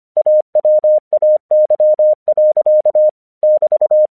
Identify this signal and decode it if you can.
Morse code, heard as a keyed tone, about 25 wpm, 610 Hz. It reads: AWAY. <BT>
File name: 1 English 25wpm / AQC02